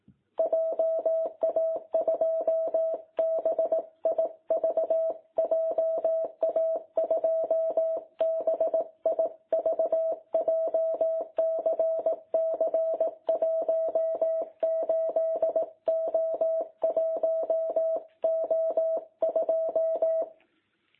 Morse code, heard as a keyed tone, about 18 wpm, 660 Hz. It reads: JA2BIV JA2BIV JCC 18O1O2
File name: jh8cbh / 2021cw